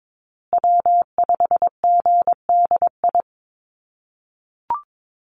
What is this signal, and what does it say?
Morse code, heard as a keyed tone, about 22 wpm, 700 Hz. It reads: W5GDI
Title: USA Call Signs Repeated 2x 22wpm